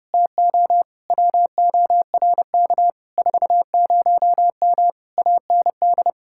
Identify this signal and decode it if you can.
Morse code, heard as a keyed tone, about 30 wpm, 700 Hz. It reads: TO WORK 40M AND
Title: Speed-Racing Common QSO Phrases - 30wpm